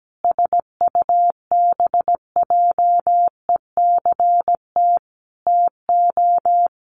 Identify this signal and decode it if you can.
Morse code, heard as a keyed tone, about 17 wpm, 700 Hz. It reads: SUBJECT TO
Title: Sentences from Top 1300 Words 17wpm